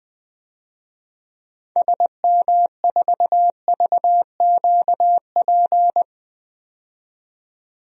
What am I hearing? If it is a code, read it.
Morse code, heard as a keyed tone, about 20 wpm, 700 Hz. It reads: SM4VQP
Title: DXCC-100 Call Signs - 6 Characters - 20wpm